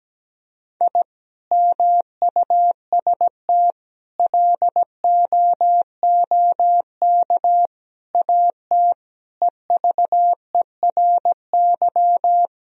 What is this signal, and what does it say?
Morse code, heard as a keyed tone, about 17 wpm, 700 Hz. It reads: I MUST LOOK AT EVERY
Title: Sentences from Top 1000 Words - Review Part 3 - 17wpm